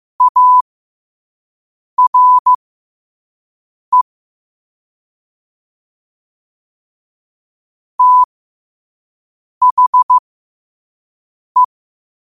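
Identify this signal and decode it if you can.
Morse code, heard as a keyed tone, about 15 wpm, spaced as for 5 wpm, 1000 Hz. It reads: ARE THE
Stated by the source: Morse quote of the day at 5 wpm.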